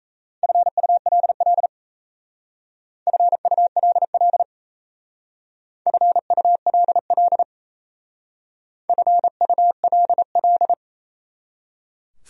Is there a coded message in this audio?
Yes: Morse code from 0.4 to 10.7 s, heard as a keyed tone, about 35 wpm, 700 Hz. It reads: FULL FULL FULL FULL